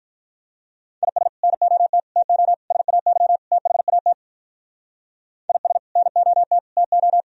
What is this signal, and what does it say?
Morse code, heard as a keyed tone, about 43 wpm, 700 Hz. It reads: IS NOT TO SAY THAT IS NOT TO
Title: Speed-Racing - Sets of 5 Words 35wpm